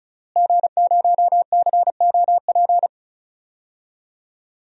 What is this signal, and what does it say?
Morse code, heard as a keyed tone, about 36 wpm, 700 Hz. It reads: G0COP
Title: DXCC-100 Call Signs - 5 Characters - Repeated 3x - 35wpm